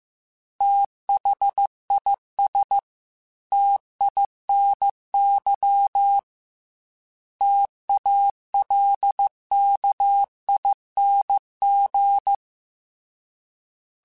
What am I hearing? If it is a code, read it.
Morse code, heard as a keyed tone, about 15 wpm, 800 Hz. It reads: THIS TINY TALKING
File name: T Runs / T Runs mp3s / T Runs @13